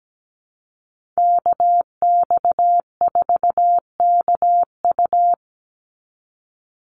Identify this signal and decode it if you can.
Morse code, heard as a keyed tone, about 17 wpm, 700 Hz. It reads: KX4KU
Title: Call Signs Repeated 2x - 17wpm